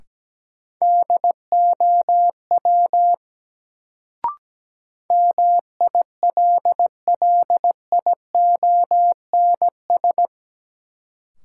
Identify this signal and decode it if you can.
Morse code, heard as a keyed tone, about 17 wpm, 700 Hz. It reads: DOW MILLIONS
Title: Sets of 5 Words - New Word Drill - 17wpm